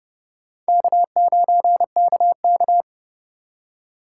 Morse code, heard as a keyed tone, about 36 wpm, 700 Hz. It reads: K9KK
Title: Speed Racing - USA 1x2 Call Signs 30wpm